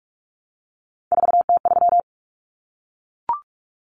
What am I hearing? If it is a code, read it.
Morse code, heard as a keyed tone, about 44 wpm, 700 Hz. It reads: 4T3
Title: Common Cut Numbers - 3-Digit 45wpm